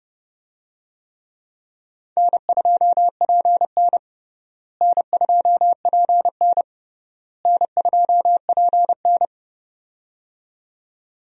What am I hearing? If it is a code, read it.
Morse code, heard as a keyed tone, about 30 wpm, 700 Hz. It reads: N2PN N2PN N2PN
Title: USA Call Signs - 4 Characters - Repeated 3x - 30wpm